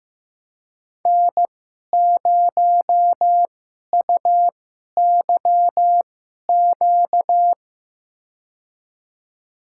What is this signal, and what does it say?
Morse code, heard as a keyed tone, about 16 wpm, spaced as for 11 wpm, 700 Hz. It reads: N0UYQ
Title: USA Call Signs - 5 Characters - 2x Character Spacing - 15wpm